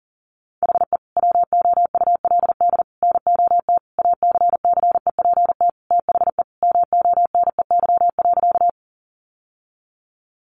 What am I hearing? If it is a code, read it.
Morse code, heard as a keyed tone, about 40 wpm, 700 Hz. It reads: HE WOULD NOT ACCEPT THE MONEY.
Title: Sentences from Top 1300 Words 40wpm